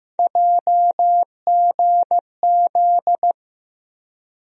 Morse code, heard as a keyed tone, about 15 wpm, 700 Hz. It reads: JGZ
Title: USA Call Sign Postfixes - 3 Letters 15wpm